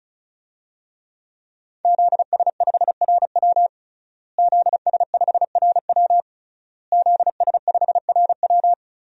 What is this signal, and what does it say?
Morse code, heard as a keyed tone, about 36 wpm, 700 Hz. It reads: ZS5RW ZS5RW ZS5RW
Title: DXCC-100 Call Signs - 5 Characters - Repeated 3x - 35wpm